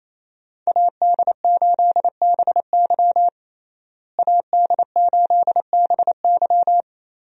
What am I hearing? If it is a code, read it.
Morse code, heard as a keyed tone, about 28 wpm, 700 Hz. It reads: AD8BY AD8BY
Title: USA Call Signs - 5 Characters - Repeated 2x - 28wpm